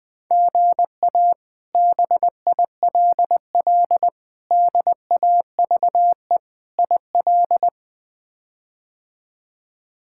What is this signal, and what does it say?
Morse code, heard as a keyed tone, about 21 wpm, 700 Hz. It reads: GA BILL DAVE IL
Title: K1USN SST QSO Practice - 20wpm